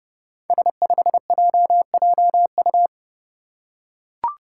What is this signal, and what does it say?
Morse code, heard as a keyed tone, about 30 wpm, 700 Hz. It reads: S5JJU
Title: Five Challenging Letter-Number Combinations 30wpm